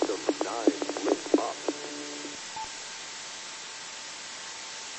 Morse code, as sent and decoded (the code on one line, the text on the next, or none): none